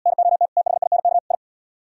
Morse code, heard as a keyed tone, about 73 wpm, 700 Hz. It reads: NOT THINK I